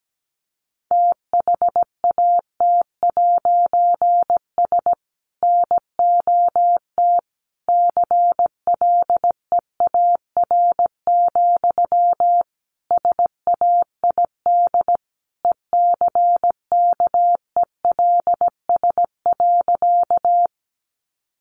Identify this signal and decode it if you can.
Morse code, heard as a keyed tone, about 17 wpm, 700 Hz. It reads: THAT'S NOT CLEAR, SAID ECKELS.
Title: A Sound Of Thunder 17wpm